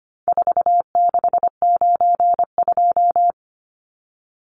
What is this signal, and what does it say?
Morse code, heard as a keyed tone, about 25 wpm, 700 Hz. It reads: 4692
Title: Numbers - 4-Digit 25wpm